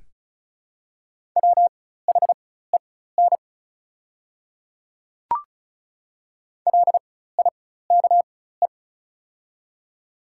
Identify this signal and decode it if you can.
Morse code, heard as a keyed tone, about 36 wpm, spaced as for 16 wpm, 700 Hz. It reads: WHEN LIKE
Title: 4x Character Spacing - Top 100 Words - 35wpm